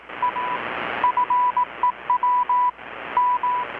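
Morse code, heard as a keyed tone, about 18 wpm, 990 Hz, with radio noise. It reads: A FEW M
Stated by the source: Morse from KPH's Night of Nights XX maritime radiotelegraphy event, received in Chiapas, Mexico about 3,800 km away on a Kenwood R-600 with a grounded 10 m long-wire antenna.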